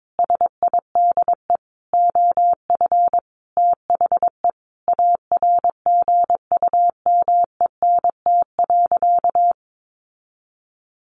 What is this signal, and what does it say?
Morse code, heard as a keyed tone, about 22 wpm, 700 Hz. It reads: SIDE OF THE ARGUMENT.
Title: Sentences from Top 1400 Words 22wpm